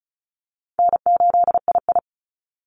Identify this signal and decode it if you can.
Morse code, heard as a keyed tone, about 34 wpm, 700 Hz. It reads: N8II
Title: Easy Call Signs Repeated 2x - 35wpm